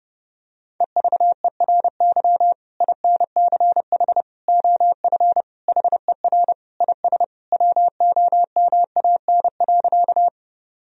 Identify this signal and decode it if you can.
Morse code, heard as a keyed tone, about 30 wpm, 700 Hz. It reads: EVERY INCH OF HER IS WOMAN.
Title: Sentences from Top 1000 Words - Review Part 3 - 30wpm